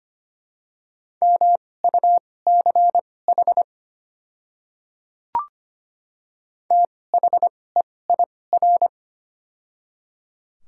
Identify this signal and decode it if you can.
Morse code, heard as a keyed tone, about 25 wpm, spaced as for 18 wpm, 700 Hz. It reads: MUCH THEIR
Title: 2x Character Spacing - Top 200 Words - 25wpm